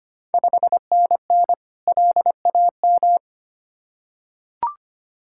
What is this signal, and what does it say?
Morse code, heard as a keyed tone, about 25 wpm, 700 Hz. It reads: 5NN LAM
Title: Ontario QSO Party Exchange 25wpm